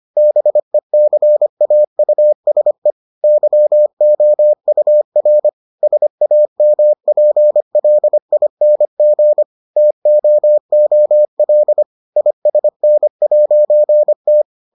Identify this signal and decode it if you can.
Morse code, heard as a keyed tone, about 25 wpm, 600 Hz. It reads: BECAUSE YOUR SAMPLING TOOL ISN'T